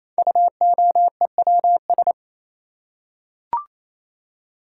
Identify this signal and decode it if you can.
Morse code, heard as a keyed tone, about 28 wpm, 700 Hz. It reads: UOEWS
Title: Code Groups - 5 Letters - 28wpm